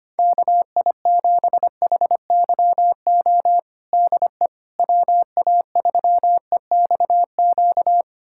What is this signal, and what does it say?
Morse code, heard as a keyed tone, about 25 wpm, 700 Hz. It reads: KI7HYO DE WA3EXQ
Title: USA Callsign DE Callsign - 6 Characters - 25wpm